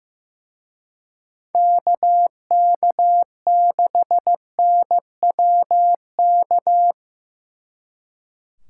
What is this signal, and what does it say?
Morse code, heard as a keyed tone, about 17 wpm, 700 Hz. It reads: KK6NWK
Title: Speed Racing - USA 2x3 Call Signs 15wpm